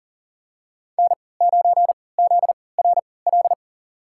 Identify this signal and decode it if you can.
Morse code, heard as a keyed tone, about 40 wpm, spaced as for 23 wpm, 700 Hz. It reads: N9ZRL
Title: USA Call Signs - 5 Characters - 3x Character Spacing - 40wpm